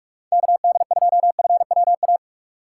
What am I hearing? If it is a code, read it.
Morse code, heard as a keyed tone, about 46 wpm, 700 Hz. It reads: KD2FWA
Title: USA Call Signs - 6 Characters - 3x Character Spacing - 45wpm